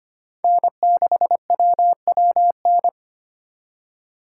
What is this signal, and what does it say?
Morse code, heard as a keyed tone, about 25 wpm, 700 Hz. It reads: N6WWN
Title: USA Call Signs - 5 Characters - 2x Character Spacing - 25wpm